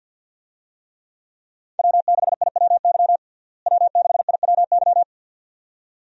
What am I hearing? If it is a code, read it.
Morse code, heard as a keyed tone, about 52 wpm, 700 Hz. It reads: W6IWY W6IWY
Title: USA Call Signs - 5 Characters - Repeated 2x - 50wpm